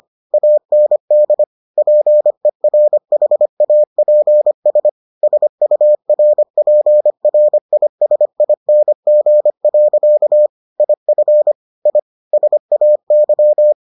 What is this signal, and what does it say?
Morse code, heard as a keyed tone, about 25 wpm, 600 Hz. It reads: AND PERHAPS SURPRISING. IF I SAY